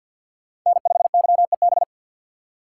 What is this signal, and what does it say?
Morse code, heard as a keyed tone, about 52 wpm, 700 Hz. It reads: N5YEB
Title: USA Call Signs - 5 Characters - 2x Character Spacing - 50wpm